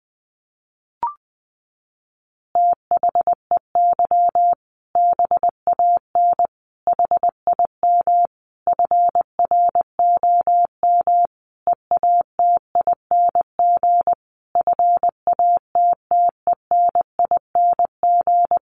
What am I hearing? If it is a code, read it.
Morse code, heard as a keyed tone, about 20 wpm, 700 Hz. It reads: THEY BAN HIM FROM EATING FATTENING
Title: Easy Phrases 20wpm